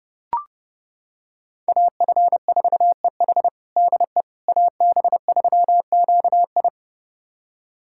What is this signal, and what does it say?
Morse code, heard as a keyed tone, about 30 wpm, 700 Hz. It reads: AF4EH DE AB3QI
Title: USA Callsign DE Callsign - 5 Characters - 30wpm